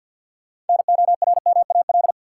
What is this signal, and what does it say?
Morse code, heard as a keyed tone, about 47 wpm, 700 Hz. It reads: NORMAL
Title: Sentences from Top 1200 Words - New Word Drill Alt - 50wpm